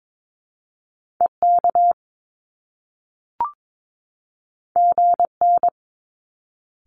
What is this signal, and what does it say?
Morse code, heard as a keyed tone, about 22 wpm, 700 Hz. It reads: EK GN
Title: Top Two Letter Combinations in Words 22wpm